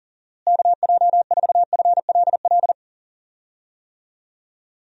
Morse code, heard as a keyed tone, about 40 wpm, 700 Hz. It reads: KJ4FLL
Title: USA Call Signs - 6 Characters - Repeated 3x - 40wpm